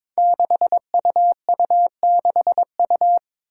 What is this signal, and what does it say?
Morse code, heard as a keyed tone, about 22 wpm, 700 Hz. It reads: 6UU6U
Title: Five Challenging Letter-Number Combinations 22wpm